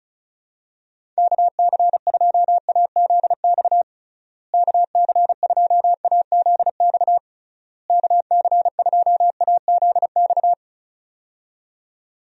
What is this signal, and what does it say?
Morse code, heard as a keyed tone, about 34 wpm, 700 Hz. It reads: KC2AZX KC2AZX KC2AZX
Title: USA Call Signs - 6 Characters - Repeated 3x - 35wpm